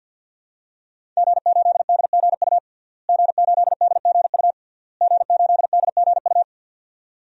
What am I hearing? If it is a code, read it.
Morse code, heard as a keyed tone, about 48 wpm, 700 Hz. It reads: G8DGU G8DGU G8DGU
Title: DXCC-100 Call Signs - 5 Characters - Repeated 3x - 50wpm